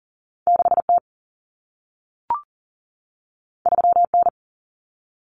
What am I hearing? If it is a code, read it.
Morse code, heard as a keyed tone, about 40 wpm, 700 Hz. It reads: 6T 3N
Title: Common Cut Numbers - 2-Digit 40wpm